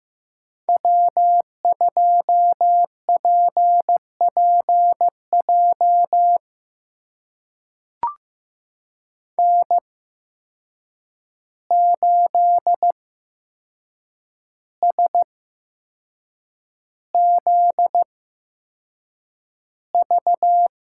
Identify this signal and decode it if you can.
Morse code, heard as a keyed tone, about 16 wpm, spaced as for 6 wpm, 700 Hz. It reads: W2PPJ N8SZV
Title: USA Call Signs - 5 Characters - 8x Character Spacing - 15wpm